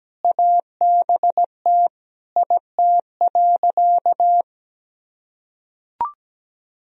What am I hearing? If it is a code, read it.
Morse code, heard as a keyed tone, about 17 wpm, 700 Hz. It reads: ABT IT.
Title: Sentences from Top 500 Words and Abbreviations 17wpm